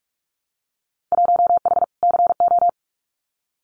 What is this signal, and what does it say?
Morse code, heard as a keyed tone, about 46 wpm, 700 Hz. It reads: 1H CO
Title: Winter Field Day Exchanges 45wpm